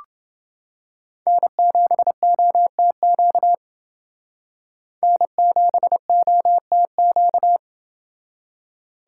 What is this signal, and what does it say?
Morse code, heard as a keyed tone, about 30 wpm, 700 Hz. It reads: N7OTQ N7OTQ